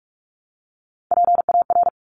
Morse code, heard as a keyed tone, about 46 wpm, 690 Hz. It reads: PAR
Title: Top Three Letter Combinations in Words 45wpm